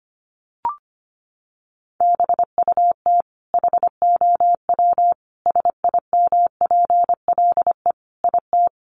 Morse code, heard as a keyed tone, about 25 wpm, 700 Hz. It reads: BUT HOW SIMPLE IT
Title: Sentences from Top 400 Words 25wpm